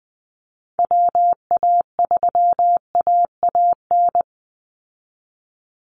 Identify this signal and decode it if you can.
Morse code, heard as a keyed tone, about 19 wpm, 700 Hz. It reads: WA3AAN
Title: Call Signs Repeated 2x - 20wpm